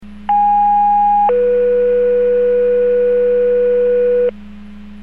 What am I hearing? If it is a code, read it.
Two-tone page, heard as a one-second tone, then a three-second tone, about 820 then 490 Hz, with noise. It is heard from 0.3 to 4.3 s.